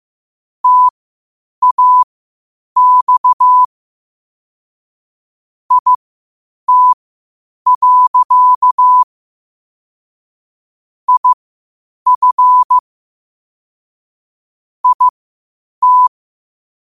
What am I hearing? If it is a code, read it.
Morse code, heard as a keyed tone, about 15 wpm, spaced as for 8 wpm, 1000 Hz. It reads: TAX IT. IF IT
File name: qotdfiles / 8wpm